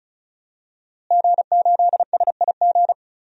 Morse code, heard as a keyed tone, about 36 wpm, 700 Hz. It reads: G8SIG
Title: DXCC-100 Call Signs - 5 Characters - Repeated 3x - 35wpm